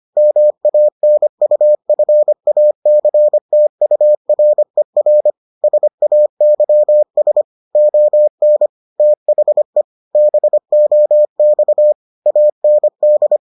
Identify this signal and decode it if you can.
Morse code, heard as a keyed tone, about 25 wpm, 600 Hz. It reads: MANUFACTURER SAYS ON THE BOX AND